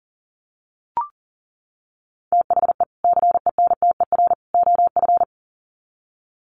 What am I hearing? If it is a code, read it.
Morse code, heard as a keyed tone, about 40 wpm, 700 Hz. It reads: THE CENTER OF
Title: Sets of 3 Words 40wpm